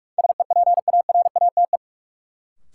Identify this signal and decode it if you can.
Morse code, heard as a keyed tone, about 44 wpm, 700 Hz. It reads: SEPARATE
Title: Sentences from Top 1000 Words - Review Part 7 - New Word Drill - 45wpm